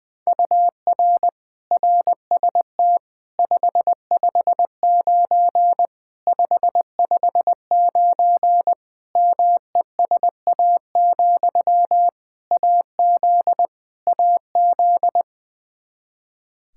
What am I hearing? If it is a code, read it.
Morse code, heard as a keyed tone, about 20 wpm, 700 Hz. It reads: UR RST 559 559 MESA, AZ AZ
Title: Practice QSO - Exchange 1 - Beginning - 20wpm